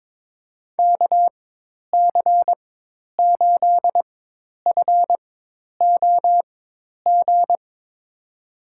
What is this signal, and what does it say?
Morse code, heard as a keyed tone, about 22 wpm, spaced as for 10 wpm, 700 Hz. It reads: KC8FOG
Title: USA Call Signs - 6 Characters - 4x Character Spacing - 22wpm